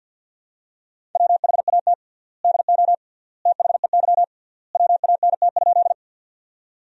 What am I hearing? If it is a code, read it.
Morse code, heard as a keyed tone, about 50 wpm, 700 Hz. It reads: WHAT DO THEY WANT?
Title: 3x Word-Spacing - Sentences from Top 200 Words - 50wpm